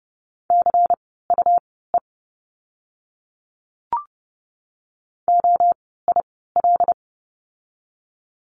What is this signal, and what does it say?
Morse code, heard as a keyed tone, about 30 wpm, spaced as for 17 wpm, 700 Hz. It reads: CUE OIL
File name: Lesson 17f - Letter u - Words 30wpm